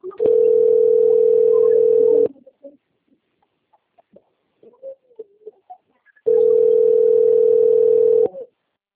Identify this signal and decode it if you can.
Ringback tone (North America), 440 + 480 Hz.